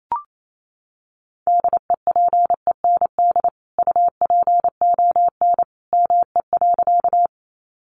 Morse code, heard as a keyed tone, about 28 wpm, 700 Hz. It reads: DEPEND UPON ME.
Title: Sentences from Top 1300 Words 28wpm